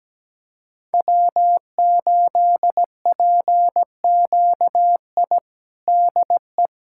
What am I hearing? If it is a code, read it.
Morse code, heard as a keyed tone, about 17 wpm, 700 Hz. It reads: W8PQI DE
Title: USA Callsign DE Callsign - 5 Characters - 17wpm